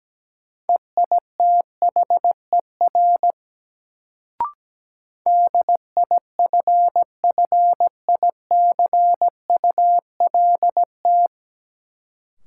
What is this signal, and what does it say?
Morse code, heard as a keyed tone, about 17 wpm, 700 Hz. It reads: EITHER DIFFICULT
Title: Sentences from Top 600 Words - New Word Drill - 17wpm